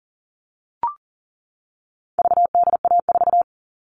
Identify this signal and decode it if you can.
Morse code, heard as a keyed tone, about 40 wpm, 700 Hz. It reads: VDA4
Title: All Cut Numbers - 4-Digit 40wpm